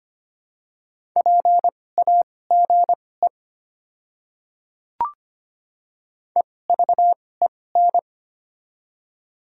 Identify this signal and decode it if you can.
Morse code, heard as a keyed tone, about 25 wpm, spaced as for 18 wpm, 700 Hz. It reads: PAGE EVEN
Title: 2x Character Spacing - Top 300 Words - 25wpm